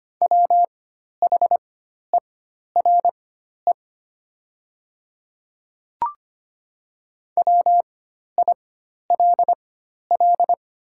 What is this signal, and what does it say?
Morse code, heard as a keyed tone, about 25 wpm, spaced as for 12 wpm, 690 Hz. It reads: WHERE WILL
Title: 4x Character Spacing - Top 500 Words - 25wpm